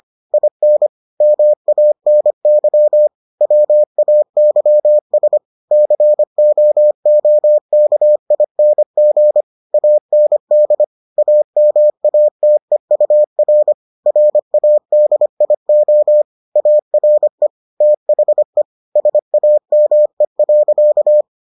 Morse code, heard as a keyed tone, about 25 wpm, 600 Hz. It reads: IN MANY WAYS COOKING AND AMATEUR RADIO ARE THE SAME.